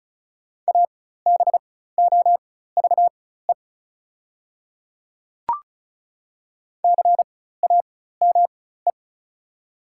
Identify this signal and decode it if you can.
Morse code, heard as a keyed tone, about 34 wpm, spaced as for 16 wpm, 690 Hz. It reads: ABOVE CAME
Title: 4x Character Spacing - Top 500 Words - 35wpm